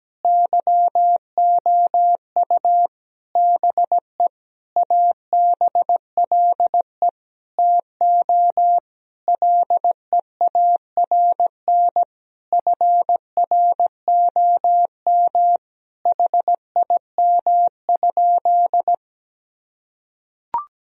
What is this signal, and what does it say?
Morse code, heard as a keyed tone, about 17 wpm, 700 Hz. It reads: YOU BE ABLE TO LEARN FROM HIM?
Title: Sentences from Top 1000 Words - Review Part 2 - 17wpm